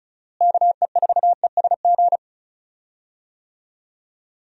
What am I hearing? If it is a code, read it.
Morse code, heard as a keyed tone, about 34 wpm, 700 Hz. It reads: KE4ESG
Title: USA Call Signs - 6 Characters - Repeated 3x - 35wpm